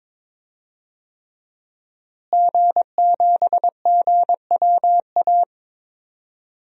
Morse code, heard as a keyed tone, about 22 wpm, 700 Hz. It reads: G7GWA
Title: DXCC-100 Call Signs - 5 Characters - Repeated 3x - 22wpm